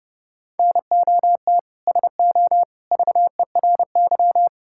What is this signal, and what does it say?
Morse code, heard as a keyed tone, about 30 wpm, 700 Hz. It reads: NOT SO VERY
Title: Sentences from Top 1600 Words 30wpm